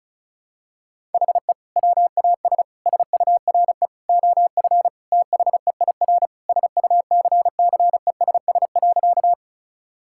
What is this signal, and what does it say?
Morse code, heard as a keyed tone, about 36 wpm, 700 Hz. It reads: HE WAS SURE OF THEIR SUCCESS.
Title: Sentences from Top 1000 Words - Review Part 6 - 35wpm